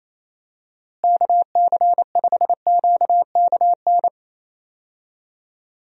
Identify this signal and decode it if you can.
Morse code, heard as a keyed tone, about 28 wpm, 700 Hz. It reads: KC5QKN
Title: USA Call Signs - 6 Characters - Repeated 3x - 28wpm